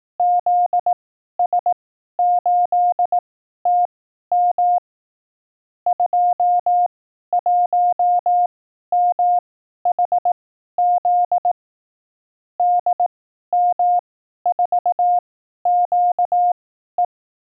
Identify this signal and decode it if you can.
Morse code, heard as a keyed tone, about 18 wpm, spaced as for 12 wpm, 710 Hz. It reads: ZS8TM 21MHZ DM4QE
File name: 12WPM test